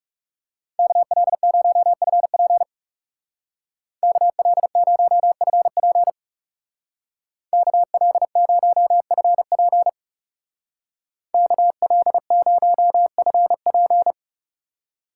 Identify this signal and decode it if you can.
Morse code, heard as a keyed tone, about 38 wpm, 700 Hz. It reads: KL0FP KL0FP KL0FP KL0FP